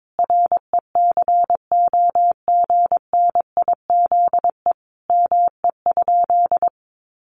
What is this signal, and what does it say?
Morse code, heard as a keyed tone, about 22 wpm, 700 Hz. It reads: RECOGNIZE ME?
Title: Sentences from Top 1400 Words 22wpm